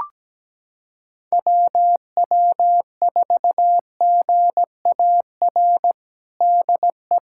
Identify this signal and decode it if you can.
Morse code, heard as a keyed tone, about 17 wpm, 700 Hz. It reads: WW4GAR DE
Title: USA Callsign DE Callsign - 6 Characters - 17wpm